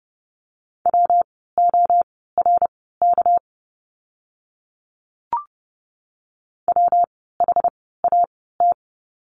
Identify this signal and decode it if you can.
Morse code, heard as a keyed tone, about 30 wpm, spaced as for 17 wpm, 700 Hz. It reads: WORK WHAT
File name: course / 30:10wpm / 230 Lesson 37f - Letter x - Words 30wpm